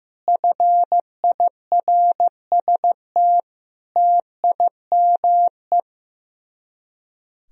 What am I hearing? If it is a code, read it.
Morse code, heard as a keyed tone, about 15 wpm, 700 Hz. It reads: FIRST TIME
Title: Speed-Racing - Sets of 4 Words 15wpm